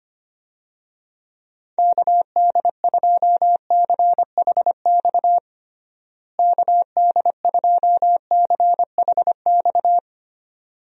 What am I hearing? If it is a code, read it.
Morse code, heard as a keyed tone, about 25 wpm, 700 Hz. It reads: KD2CHX KD2CHX
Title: USA Call Signs - 6 Characters - Repeated 2x - 25wpm